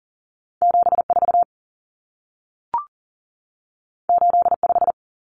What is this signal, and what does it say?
Morse code, heard as a keyed tone, about 40 wpm, 700 Hz. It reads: 74 85
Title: Numbers - 2-Digit 40wpm